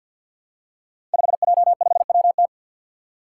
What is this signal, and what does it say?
Morse code, heard as a keyed tone, about 46 wpm, 700 Hz. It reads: 5J5WT
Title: Five Challenging Letter-Number Combinations 50wpm